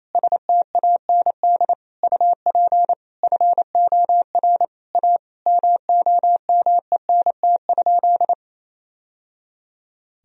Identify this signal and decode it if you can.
Morse code, heard as a keyed tone, about 28 wpm, 700 Hz. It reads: STAND UP FOR A MOMENT?
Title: Sentences from Top 1000 Words - Review Part 1 - 28wpm